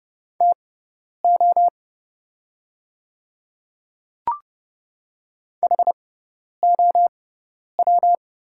Morse code, heard as a keyed tone, about 30 wpm, spaced as for 10 wpm, 690 Hz. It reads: TO HOW
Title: 6x Character Spacing - Top 200 Words - 30wpm